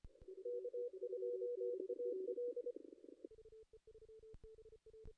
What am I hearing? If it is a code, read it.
Morse code, heard as a keyed tone, about 25 wpm, 490 Hz. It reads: K2AL
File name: contests / ss-cw-k5tr-op-n5rz -2300